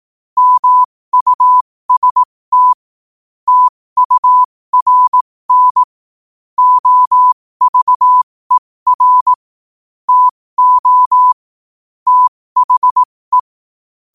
Morse code, heard as a keyed tone, about 18 wpm, spaced as for 15 wpm, 1000 Hz. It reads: MUST TURN OVER TO THE
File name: qotdfiles / 15wpm